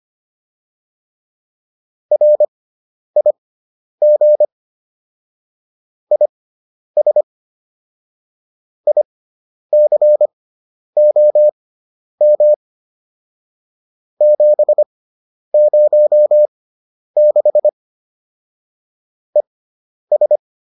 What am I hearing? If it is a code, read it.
Morse code, heard as a keyed tone, about 25 wpm, spaced as for 10 wpm, 600 Hz. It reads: RIG IS ICOM 706 ES
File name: QSO302 10